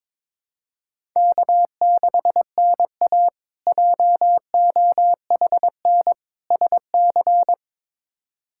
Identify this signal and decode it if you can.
Morse code, heard as a keyed tone, about 22 wpm, 700 Hz. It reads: K6NA JOHN SC
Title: K1USN SST QSO Practice - 22wpm